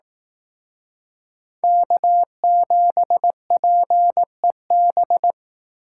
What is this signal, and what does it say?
Morse code, heard as a keyed tone, about 19 wpm, 700 Hz. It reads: K7PEB
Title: Speed Racing - USA Call Signs 15wpm